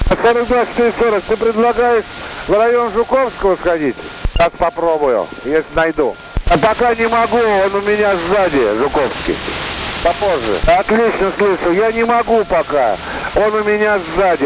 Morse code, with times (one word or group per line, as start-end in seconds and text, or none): none